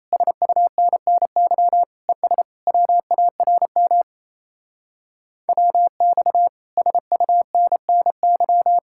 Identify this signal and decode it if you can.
Morse code, heard as a keyed tone, about 33 wpm, 700 Hz. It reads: SUNNY ES WARM WX SUNNY
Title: Speed-Racing Common QSO Phrases Encore - 28wpm